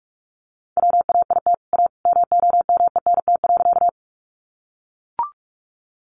Morse code, heard as a keyed tone, about 44 wpm, 700 Hz. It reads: WAIT A MOMENT.
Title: Sentences from Top 1000 Words 45wpm